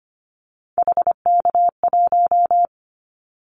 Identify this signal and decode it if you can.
Morse code, heard as a keyed tone, about 25 wpm, 700 Hz. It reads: HK1
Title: Callsign Prefixes for Top 100 DXCC - 25wpm